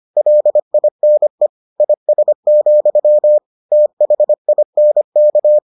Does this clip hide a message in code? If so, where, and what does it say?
Yes: Morse code from 0.2 to 5.6 s, heard as a keyed tone, about 25 wpm, 600 Hz. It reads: LINE IS, THINK